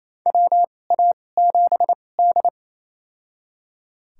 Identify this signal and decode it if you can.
Morse code, heard as a keyed tone, about 28 wpm, spaced as for 20 wpm, 700 Hz. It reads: WA7D
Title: USA Call Signs - 4 Characters - 2x Character Spacing - 28wpm